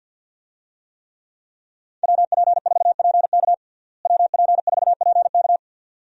Morse code, heard as a keyed tone, about 50 wpm, 700 Hz. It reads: WP4PK WP4PK
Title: USA Call Signs - 5 Characters - Repeated 2x - 50wpm